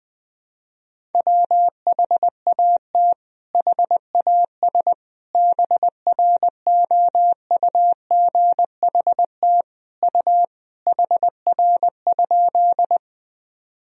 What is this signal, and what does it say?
Morse code, heard as a keyed tone, about 21 wpm, 700 Hz. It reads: WHAT HAS BROUGHT U HR?
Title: Sentences from Top 500 Words and Abbreviations 20wpm